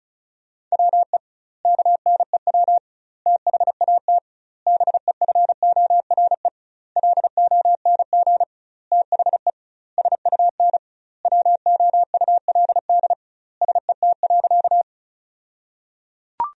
Morse code, heard as a keyed tone, about 34 wpm, 700 Hz. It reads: WE KNEW THAT BEFORE LONG THE SUN WOULD SET.
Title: 2x Word-Spacing - Sentences from Top 400 Words - 35wpm